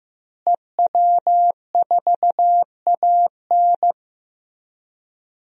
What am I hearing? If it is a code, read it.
Morse code, heard as a keyed tone, about 15 wpm, 700 Hz. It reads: EW4AN
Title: DXCC-100 Call Signs - 5 Characters - Repeated 3x - 15wpm